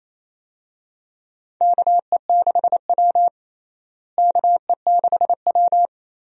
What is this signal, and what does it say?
Morse code, heard as a keyed tone, about 28 wpm, 700 Hz. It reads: KE6W KE6W
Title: USA Call Signs - 4 Characters - Repeated 2x - 28wpm